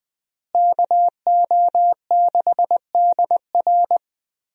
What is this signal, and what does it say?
Morse code, heard as a keyed tone, about 21 wpm, 700 Hz. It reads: KO6DR
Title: USA Call Signs - 5 Characters - Repeated 2x - 20wpm